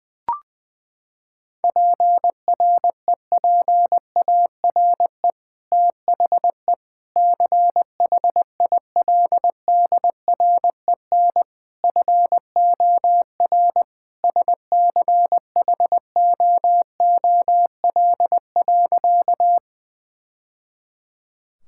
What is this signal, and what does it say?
Morse code, heard as a keyed tone, about 20 wpm, 700 Hz. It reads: PREPARE THE CHILDREN FOR SCHOOL.
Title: Sentences from Top 1000 Words - Review Part 6 - 20wpm